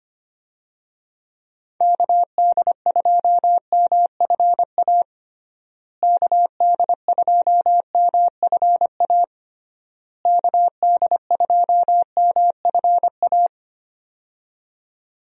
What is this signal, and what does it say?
Morse code, heard as a keyed tone, about 25 wpm, 700 Hz. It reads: KD2MFA KD2MFA KD2MFA
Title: USA Call Signs - 6 Characters - Repeated 3x - 25wpm